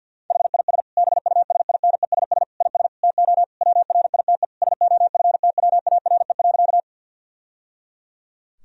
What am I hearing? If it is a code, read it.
Morse code, heard as a keyed tone, about 51 wpm, 700 Hz. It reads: HIS BUSINESS IS TO WRITE SOFTWARE.
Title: Sentences from Top 1000 Words - Review Part 3 - 50wpm